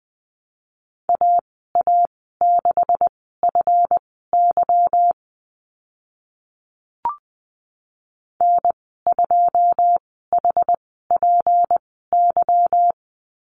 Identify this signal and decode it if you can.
Morse code, heard as a keyed tone, about 20 wpm, spaced as for 15 wpm, 700 Hz. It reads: AA6FY N2HPY
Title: 173 Lesson 28g - Number 9 - Call Signs 20wpm